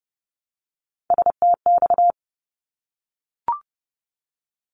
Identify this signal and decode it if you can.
Morse code, heard as a keyed tone, about 30 wpm, 700 Hz. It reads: STX
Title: ARRL Sections 30wpm